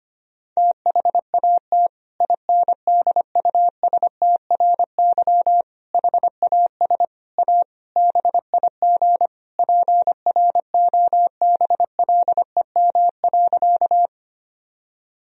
Morse code, heard as a keyed tone, about 25 wpm, 700 Hz. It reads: THAT INDUSTRY HAS A BIG PROBLEM.
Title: Sentences from Top 1000 Words - Review Part 4 - 25wpm